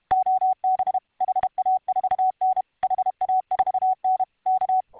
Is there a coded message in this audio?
Yes: Morse code from 0.1 to 4.8 s, heard as a keyed tone, about 31 wpm, 750 Hz. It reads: OB HA4N HA4N K